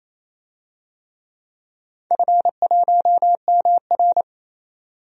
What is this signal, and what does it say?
Morse code, heard as a keyed tone, about 28 wpm, 700 Hz. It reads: F1MR
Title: DXCC-100 Call Signs - 4 Characters - 28wpm